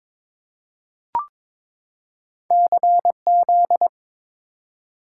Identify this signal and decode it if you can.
Morse code, heard as a keyed tone, about 22 wpm, 700 Hz. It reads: CZ
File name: practice / 22wpm / Two Letter-Number Combinations - 22wpm